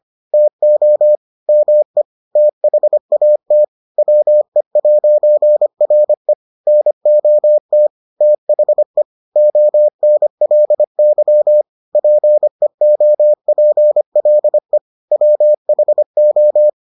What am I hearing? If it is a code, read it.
Morse code, heard as a keyed tone, about 25 wpm, 600 Hz. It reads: TO ME THAT WE'RE NOT THE ONLY PEOPLE WHO